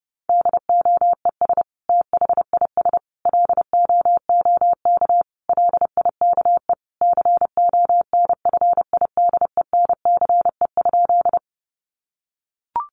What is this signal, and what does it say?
Morse code, heard as a keyed tone, about 30 wpm, 700 Hz. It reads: DOES THIS LOOK LIKE CONFIDENCE?
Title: Sentences from Top 1500 Words 30wpm